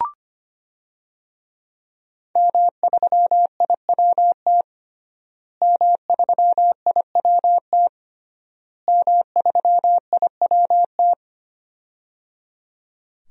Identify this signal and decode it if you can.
Morse code, heard as a keyed tone, about 25 wpm, 700 Hz. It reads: M3IWT M3IWT M3IWT
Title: DXCC-100 Call Signs - 5 Characters - Repeated 3x - 25wpm